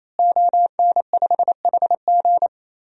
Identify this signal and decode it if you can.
Morse code, heard as a keyed tone, about 28 wpm, 700 Hz. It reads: ON5HG
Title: DXCC-100 Call Signs - 5 Characters - Repeated 3x - 28wpm